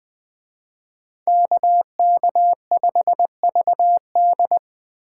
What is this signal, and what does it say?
Morse code, heard as a keyed tone, about 21 wpm, 700 Hz. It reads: KK5VD
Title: Speed Racing - USA 2x2 Call Signs 17wpm